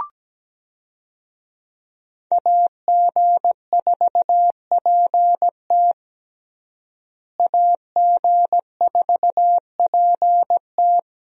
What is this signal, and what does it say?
Morse code, heard as a keyed tone, about 17 wpm, 700 Hz. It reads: AG4PT AG4PT
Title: USA Call Signs - 5 Characters - Repeated 2x - 17wpm